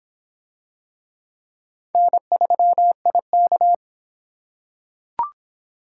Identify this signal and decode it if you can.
Morse code, heard as a keyed tone, about 25 wpm, 700 Hz. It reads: N3IK